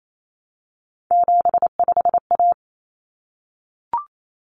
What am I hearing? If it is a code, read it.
Morse code, heard as a keyed tone, about 28 wpm, 700 Hz. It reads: 75A
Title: Common Cut Numbers - 3-Digit 28wpm